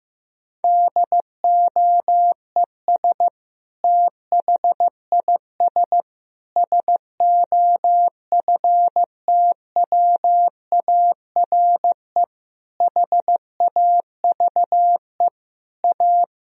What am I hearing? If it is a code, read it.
Morse code, heard as a keyed tone, about 15 wpm, 700 Hz. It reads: DOES THIS SOFTWARE HAVE A
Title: Sentences from Top 1000 Words - Review Part 3 - 15wpm